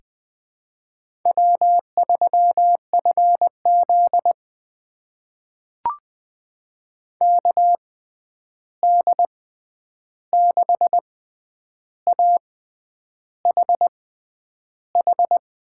Morse code, heard as a keyed tone, about 21 wpm, spaced as for 9 wpm, 700 Hz. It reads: W3FZ KD6AHH
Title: USA Call Signs - 6x Character Spacing - 20wpm